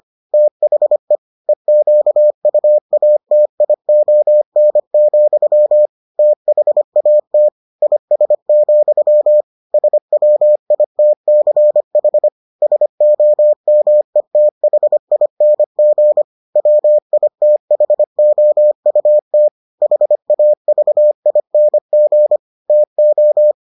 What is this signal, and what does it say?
Morse code, heard as a keyed tone, about 25 wpm, 600 Hz. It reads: THE EQUATION, THAT IS, SWITCH SOMETHING WITHOUT HAVING TO